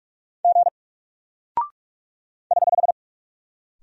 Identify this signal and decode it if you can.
Morse code, heard as a keyed tone, about 44 wpm, 700 Hz. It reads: G <HH>
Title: Single Character (Letter:Number:Punctuation:Prosign) Alt 45wpm